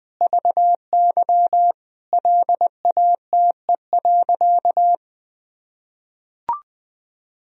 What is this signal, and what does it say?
Morse code, heard as a keyed tone, about 21 wpm, 700 Hz. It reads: VY LATE.
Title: Sentences from Top 300 Words and Abbreviations 20wpm